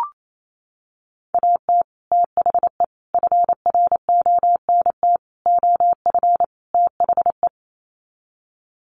Morse code, heard as a keyed tone, about 28 wpm, 700 Hz. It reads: AT THE FRONT OF THE
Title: Sets of 5 Words 28wpm